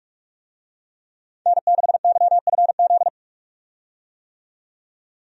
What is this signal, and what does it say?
Morse code, heard as a keyed tone, about 44 wpm, 700 Hz. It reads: N6YFZ
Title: USA Call Signs - 5 Characters - 45wpm